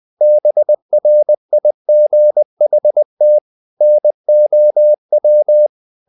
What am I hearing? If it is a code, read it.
Morse code, heard as a keyed tone, about 20 wpm, 600 Hz. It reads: BRIGHT NOW
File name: the old man and the sea part 2 20wpm